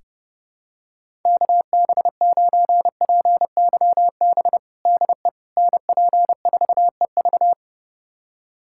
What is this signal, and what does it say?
Morse code, heard as a keyed tone, about 32 wpm, 700 Hz. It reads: KB9PYB DE NP4EV
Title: USA Callsign DE Callsign 30wpm